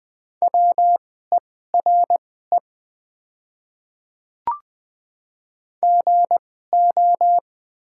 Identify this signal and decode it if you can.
Morse code, heard as a keyed tone, about 20 wpm, spaced as for 14 wpm, 700 Hz. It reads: WERE GO
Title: 2x Character Spacing - Top 100 Words - 20wpm